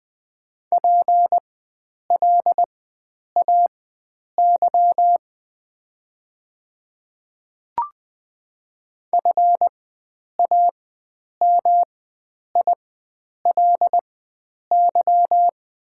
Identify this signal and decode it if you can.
Morse code, heard as a keyed tone, about 20 wpm, spaced as for 9 wpm, 700 Hz. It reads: PLAY FAMILY
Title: 4x Character Spacing - Top 500 Words - 20wpm